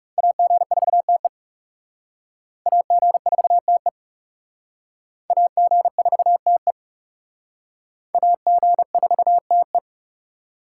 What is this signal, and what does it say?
Morse code, heard as a keyed tone, about 37 wpm, 700 Hz. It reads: AG4TE AG4TE AG4TE AG4TE
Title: Speed Racing - USA 2x2 Call Signs 30wpm